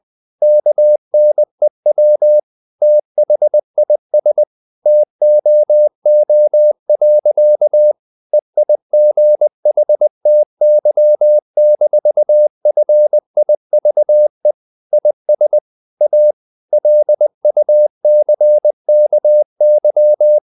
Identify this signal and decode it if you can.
Morse code, heard as a keyed tone, about 20 wpm, 600 Hz. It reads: KNEW THIS TOO. EIGHTY-FIVE IS A LUCKY
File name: the old man and the sea / the old man and the sea part 1 20wpm